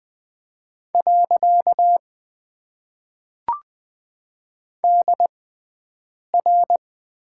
Morse code, heard as a keyed tone, about 20 wpm, spaced as for 7 wpm, 700 Hz. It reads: . DR
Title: Top 90 QSO Elements - 6x Character Spacing - 20wpm